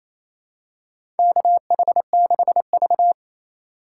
Morse code, heard as a keyed tone, about 28 wpm, 700 Hz. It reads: KH6V
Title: USA Call Signs - 3x Character Spacing - 28wpm